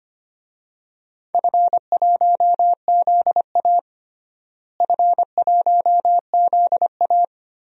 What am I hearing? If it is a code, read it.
Morse code, heard as a keyed tone, about 25 wpm, 700 Hz. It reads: F1ZA F1ZA
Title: DXCC-100 Call Signs - 4 Characters - Repeated 3x - 25wpm